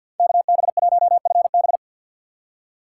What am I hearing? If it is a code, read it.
Morse code, heard as a keyed tone, about 49 wpm, 700 Hz. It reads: KB1FB
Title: USA Call Signs - 5 Characters - 3x Character Spacing - 50wpm